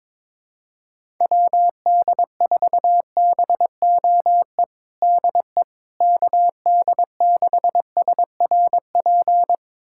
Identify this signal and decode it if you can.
Morse code, heard as a keyed tone, about 22 wpm, 700 Hz. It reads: WD4BOE DE KD6SRP
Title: USA Callsign DE Callsign 22wpm